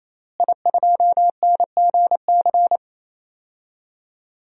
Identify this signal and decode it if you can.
Morse code, heard as a keyed tone, about 28 wpm, 700 Hz. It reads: I2NGC
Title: DXCC-100 Call Signs - 5 Characters - Repeated 3x - 28wpm